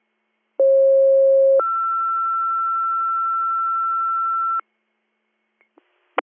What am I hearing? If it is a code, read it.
Two-tone page, heard as a one-second tone, then a three-second tone, about 540 then 1340 Hz. It is heard from 0.6 to 4.6 s.